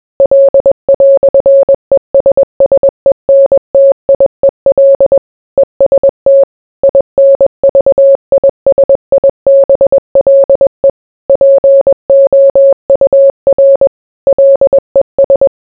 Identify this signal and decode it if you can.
Morse code, heard as a keyed tone, about 21 wpm, 550 Hz. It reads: L"ESSENTIEL EST INVISIBLE POUR LES